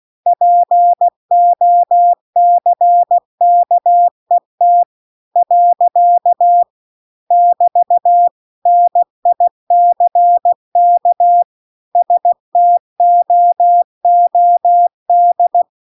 Morse code, heard as a keyed tone, about 16 wpm, 700 Hz. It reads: POCKET . <BT> NICK STOOD
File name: CW-The-Big-Twoheaded-River-2-16WPM